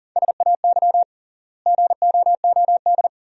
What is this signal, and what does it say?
Morse code, heard as a keyed tone, about 40 wpm, 700 Hz. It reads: SAY GOOD
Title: 3x Word-Spacing - Sentences from Top 300 Words - 40wpm